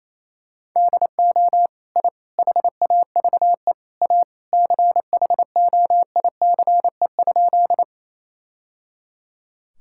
Morse code, heard as a keyed tone, about 28 wpm, 700 Hz. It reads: DO I HAVE A CHOICE?
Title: Sentences from Top 1000 Words - Review Part 6 - 28wpm